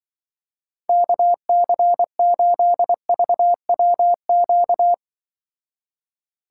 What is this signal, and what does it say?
Morse code, heard as a keyed tone, about 24 wpm, 700 Hz. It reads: KC8VWQ
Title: Speed Racing - USA 2x3 Call Signs 20wpm